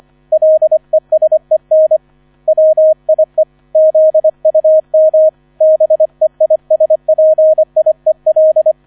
Morse code, heard as a keyed tone, about 25 wpm, 630 Hz. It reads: LESEN WIE ZUM BEISPIEL